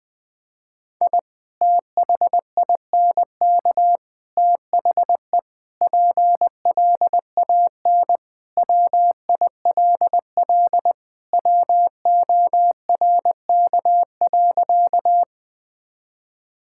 Morse code, heard as a keyed tone, about 20 wpm, 700 Hz. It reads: I THINK THE PLAN WILL WORK.
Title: Sentences with 3-Word Phrases - 20wpm